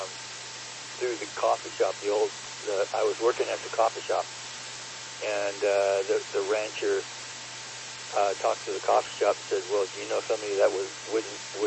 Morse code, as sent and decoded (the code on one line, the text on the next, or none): none